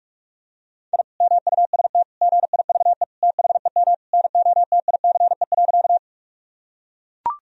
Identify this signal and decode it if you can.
Morse code, heard as a keyed tone, about 41 wpm, 700 Hz. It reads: I MUST GIVE THEM NOTICE.
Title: Sentences from Top 1000 Words - Review Part 2 - 45wpm